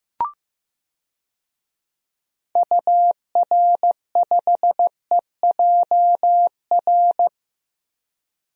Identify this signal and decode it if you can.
Morse code, heard as a keyed tone, about 15 wpm, 700 Hz. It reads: UR5EJR
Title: DXCC-100 Call Signs - 6 Characters - 15wpm